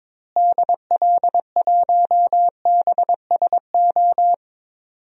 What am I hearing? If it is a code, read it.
Morse code, heard as a keyed tone, about 22 wpm, 700 Hz. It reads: DL1BSO
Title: DXCC-100 Call Signs - 6 Characters - Repeated 2x - 22wpm